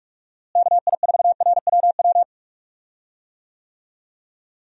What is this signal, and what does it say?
Morse code, heard as a keyed tone, about 46 wpm, 700 Hz. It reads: KI4RWW
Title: USA Call Signs - 6 Characters - Repeated 2x - 45wpm